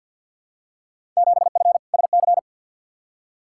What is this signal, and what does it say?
Morse code, heard as a keyed tone, about 52 wpm, 700 Hz. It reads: 7F SC
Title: Field Day Exchanges Encore - 50wpm